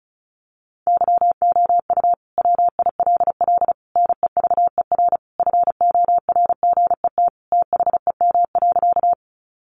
Morse code, heard as a keyed tone, about 34 wpm, 700 Hz. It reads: YOU WILL NEVER FORGET THEM.
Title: Sentences from Top 1200 Words 35wpm